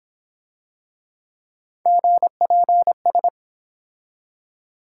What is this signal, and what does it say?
Morse code, heard as a keyed tone, about 26 wpm, 700 Hz. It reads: GPS